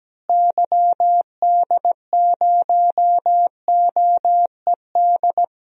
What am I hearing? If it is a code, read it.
Morse code, heard as a keyed tone, about 17 wpm, 700 Hz. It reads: YD0OED
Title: DXCC-100 Call Signs - 6 Characters - Repeated 3x - 17wpm